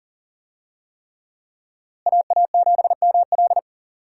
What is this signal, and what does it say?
Morse code, heard as a keyed tone, about 40 wpm, 700 Hz. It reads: AA7ML
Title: USA Call Signs - 5 Characters - Repeated 3x - 40wpm